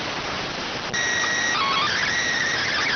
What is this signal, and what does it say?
SSTV (slow-scan television), PD 290.